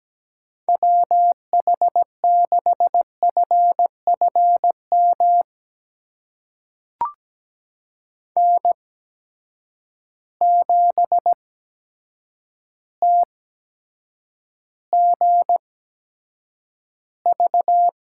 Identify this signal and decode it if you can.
Morse code, heard as a keyed tone, about 17 wpm, spaced as for 8 wpm, 700 Hz. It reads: WH6FFM N7TGV
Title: USA Call Signs - 8x Character Spacing - 17wpm